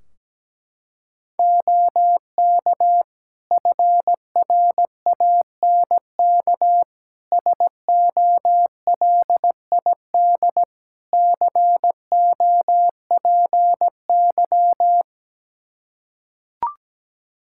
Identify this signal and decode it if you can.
Morse code, heard as a keyed tone, about 17 wpm, 700 Hz. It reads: OK FRANK SOLID COPY
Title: Practice QSO - Exchange 3 - More Info - 17wpm